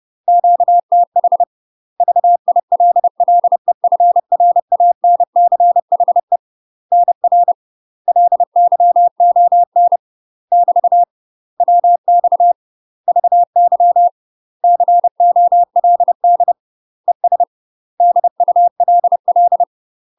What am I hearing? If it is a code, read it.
Morse code, heard as a keyed tone, about 30 wpm, 700 Hz. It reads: QTH VILLEFRANCHE NR LYON <BT> WX VY COLD ES DULL
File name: UK MASTER QSOs TRAINING / 30WPM / UK MASTER QSOs 0014